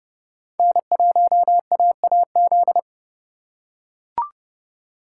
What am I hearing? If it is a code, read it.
Morse code, heard as a keyed tone, about 31 wpm, 700 Hz. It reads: N1AAZ